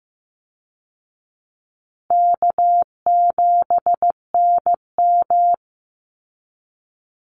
Morse code, heard as a keyed tone, about 15 wpm, 700 Hz. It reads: K7NM
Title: Easy Call Signs Repeated 2x - 15wpm